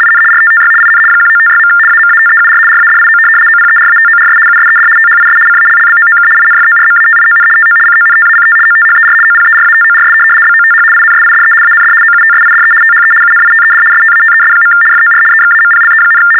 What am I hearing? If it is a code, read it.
RTTY, about 75 baud. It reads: IN RTTY IN 1956. AMATEUR RADIO OPERATORS USED THIS CALLBOOK INFORMATION TO CONTACT OTHER OPERATORS BOTH INSIDE AND OUTSIDE THE UNITED STATES. FOR EXAMPLE,
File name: teletype-RTTY-75